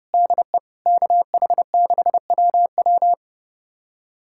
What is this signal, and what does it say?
Morse code, heard as a keyed tone, about 30 wpm, 700 Hz. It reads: DE KH6WW
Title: USA Callsign DE Callsign - 5 Characters - 30wpm